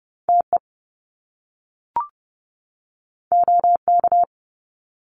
Morse code, heard as a keyed tone, about 30 wpm, 700 Hz. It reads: TE OK
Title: Top Two Letter Combinations in Words 30wpm